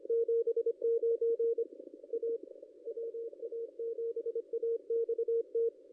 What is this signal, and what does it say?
Morse code, heard as a keyed tone, about 25 wpm, 460 Hz, with radio noise. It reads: 79 A WA7AXT
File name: contests / ss-cw-k5tr-op-n5rz -1800